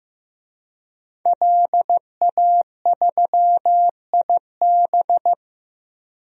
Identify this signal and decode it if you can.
Morse code, heard as a keyed tone, about 15 wpm, 700 Hz. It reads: LA3IB
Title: DXCC-100 Call Signs - 5 Characters - Repeated 3x - 15wpm